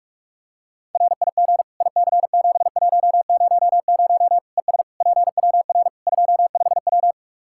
Morse code, heard as a keyed tone, about 46 wpm, 700 Hz. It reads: RIG IC7100 ES PWR 25W
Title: K1USN SST QSO Practice 45wpm